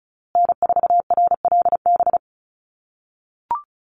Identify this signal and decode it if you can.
Morse code, heard as a keyed tone, about 34 wpm, 700 Hz. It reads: N4RLB
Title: USA Call Signs Repeated 2x 35wpm